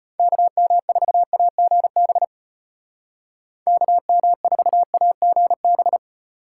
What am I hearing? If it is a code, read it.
Morse code, heard as a keyed tone, about 35 wpm, 700 Hz. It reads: KM4AGB KM4AGB